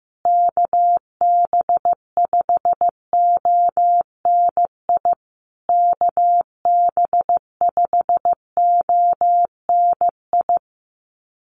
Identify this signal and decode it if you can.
Morse code, heard as a keyed tone, about 15 wpm, 700 Hz. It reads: KB5ONI KB5ONI
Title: USA Call Signs Repeated 3x 15wpm